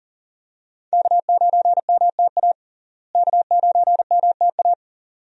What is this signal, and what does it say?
Morse code, heard as a keyed tone, about 37 wpm, 700 Hz. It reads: K9MTA K9MTA
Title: USA Call Signs - 5 Characters - Repeated 2x - 40wpm